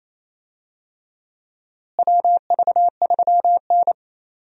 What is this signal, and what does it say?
Morse code, heard as a keyed tone, about 27 wpm, 700 Hz. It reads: WV3N
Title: USA Call Signs - 4 Characters - Repeated 3x - 28wpm